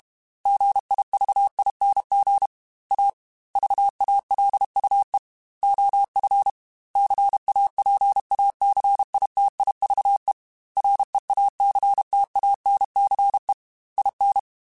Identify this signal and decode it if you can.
Morse code, heard as a keyed tone, about 32 wpm, 800 Hz. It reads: GIVING A VALUE OF CAPACITIVE REACTANCE IN
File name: Technical Subjects / Technical Subjects-mp3s / XC @30